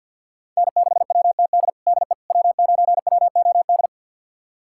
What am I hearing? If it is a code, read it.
Morse code, heard as a keyed tone, about 53 wpm, 700 Hz. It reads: N6WTD DE W9WOD
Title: USA Callsign DE Callsign - 5 Characters - 50wpm